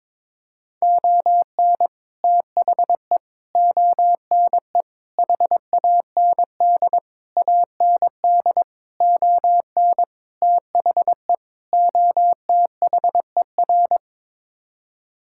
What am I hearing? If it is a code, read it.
Morse code, heard as a keyed tone, about 22 wpm, 700 Hz. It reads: ON THE ONE HAND AND ON THE OTHER
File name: practice / 22wpm / Sets of 8 Words - 22wpm